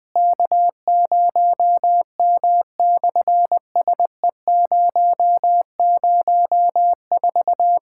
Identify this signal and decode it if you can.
Morse code, heard as a keyed tone, about 20 wpm, 690 Hz. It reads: K0M/SE004
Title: SOTA QSO Practice 20wpm